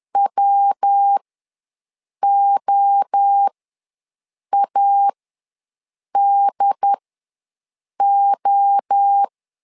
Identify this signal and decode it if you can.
Morse code, heard as a keyed tone, about 10 wpm, spaced as for 6 wpm, 800 Hz. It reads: WOADO
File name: LEZIONI / 3 MWRO / 3 MWRO ws 10wpm 2310